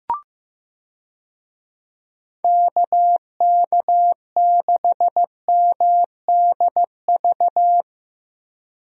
Morse code, heard as a keyed tone, about 15 wpm, 700 Hz. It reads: KK6MDV
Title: USA Call Signs - 6 Characters - Repeated 3x - 15wpm